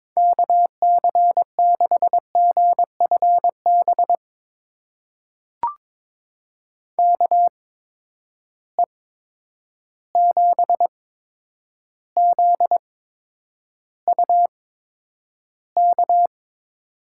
Morse code, heard as a keyed tone, about 22 wpm, spaced as for 9 wpm, 700 Hz. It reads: KC6GFB KE7ZUK
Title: USA Call Signs - 6 Characters - 8x Character Spacing - 22wpm